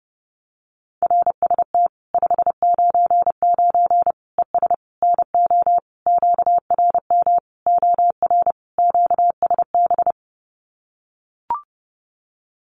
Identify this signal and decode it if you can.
Morse code, heard as a keyed tone, about 30 wpm, 700 Hz. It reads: RST 599 ES NO QRM OR QSB
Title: Common QSO Phrases 30wpm